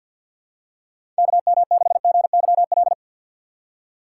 Morse code, heard as a keyed tone, about 48 wpm, 700 Hz. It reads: KM6GYL
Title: USA Call Signs - 6 Characters - 6x Character Spacing - 50wpm